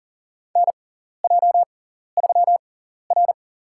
Morse code, heard as a keyed tone, about 40 wpm, spaced as for 14 wpm, 700 Hz. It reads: NJ3R
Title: USA Call Signs - 4 Characters - 6x Character Spacing - 40wpm